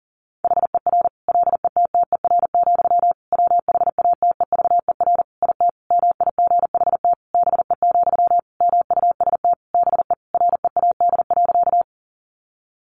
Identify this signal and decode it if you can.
Morse code, heard as a keyed tone, about 40 wpm, 700 Hz. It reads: HER LETTER, WHATEVER IT MIGHT BE, MUST BE READ.
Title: Sentences from Top 1100 Words 40wpm